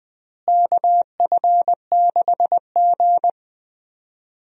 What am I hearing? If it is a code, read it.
Morse code, heard as a keyed tone, about 21 wpm, 700 Hz. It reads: KF6G
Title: USA Call Signs - 4 Characters - Repeated 3x - 20wpm